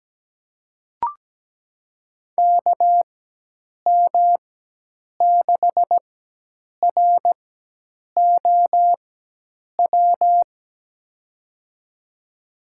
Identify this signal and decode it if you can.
Morse code, heard as a keyed tone, about 17 wpm, spaced as for 8 wpm, 700 Hz. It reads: KM6ROW
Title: USA Call Signs - 6 Characters - 4x Character Spacing - 17wpm